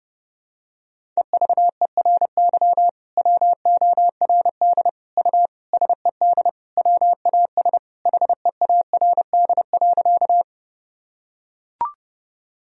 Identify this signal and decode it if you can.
Morse code, heard as a keyed tone, about 30 wpm, 700 Hz. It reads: EVERY WORD U SED WAS HEARD.
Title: Sentences from Top 400 Words and Abbreviations 30wpm